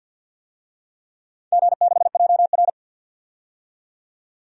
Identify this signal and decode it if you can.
Morse code, heard as a keyed tone, about 53 wpm, 700 Hz. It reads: G6JR
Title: DXCC-100 Call Signs - 4 Characters - 50wpm